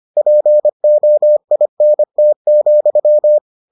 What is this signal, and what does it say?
Morse code, heard as a keyed tone, about 25 wpm, 600 Hz. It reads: POINT,